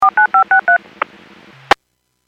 DTMF, heard as telephone touch tones, tones of about 110 ms. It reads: 49563
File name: WPS / 02436 14-15 WPS 451.25 WPS